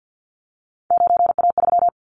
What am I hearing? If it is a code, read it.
Morse code, heard as a keyed tone, about 50 wpm, 700 Hz. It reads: 9A3
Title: Callsign Prefixes for Top 50 DXCC - 50wpm